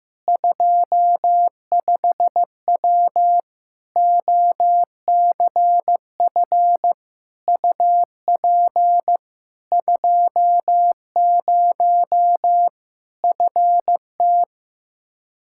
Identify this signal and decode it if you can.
Morse code, heard as a keyed tone, about 15 wpm, 700 Hz. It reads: 25W OCF UP 20 FT
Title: K1USN SST QSO Practice 15wpm